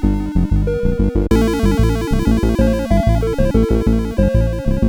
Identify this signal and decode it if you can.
There is no code.